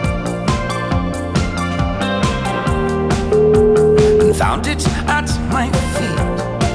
Ringback tone (Europe, 425 Hz), with noise.